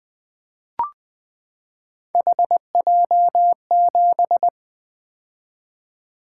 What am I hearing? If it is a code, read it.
Morse code, heard as a keyed tone, about 20 wpm, 700 Hz. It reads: HJ7
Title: Three Letter-Number Combinations - 20wpm